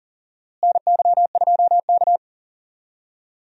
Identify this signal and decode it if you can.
Morse code, heard as a keyed tone, about 40 wpm, 700 Hz. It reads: NY2K
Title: USA Call Signs - 4 Characters - Repeated 3x - 40wpm